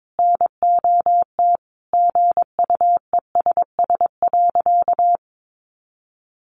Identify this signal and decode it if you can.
Morse code, heard as a keyed tone, about 22 wpm, 700 Hz. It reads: NOT GUESS.
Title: Sentences from Top 1500 Words 22wpm